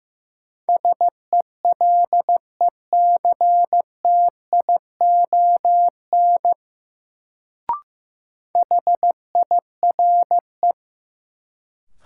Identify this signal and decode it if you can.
Morse code, heard as a keyed tone, about 15 wpm, 700 Hz. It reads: SELECTION HIRE
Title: Sentences from Top 1400 Words - New Word Drill - 15wpm